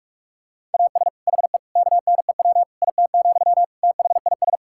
Morse code, heard as a keyed tone, about 41 wpm, 700 Hz. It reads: AS HE KNEW IT, THIS
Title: Sentences from Top 1600 Words 45wpm